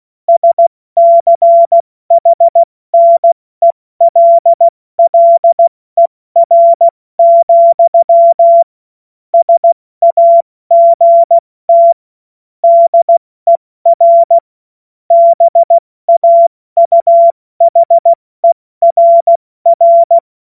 Morse code, heard as a keyed tone, about 16 wpm, 680 Hz. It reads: SCHNELLER, SAGT DER BAUHERR
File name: Morsekurs / Witze / Witz 20. Januar 2026 70 F80